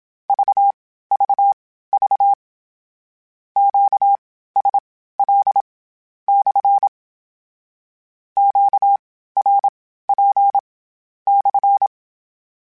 Morse code, heard as a keyed tone, about 27 wpm, spaced as for 14 wpm, 800 Hz. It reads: VVV QSL/ QRP/
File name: Q Calls / Q Calls-mp3s / Q Calls SPSL1 @25